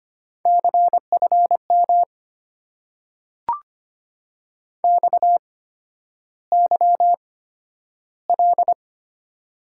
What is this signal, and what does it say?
Morse code, heard as a keyed tone, about 25 wpm, spaced as for 11 wpm, 700 Hz. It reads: CFM XYL
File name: Top 90 QSO Elements - 8x Character Spacing - 25wpm